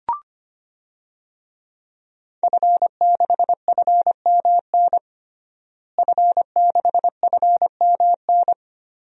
Morse code, heard as a keyed tone, about 25 wpm, 700 Hz. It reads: F6FMN F6FMN
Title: DXCC-100 Call Signs - 5 Characters - Repeated 3x - 25wpm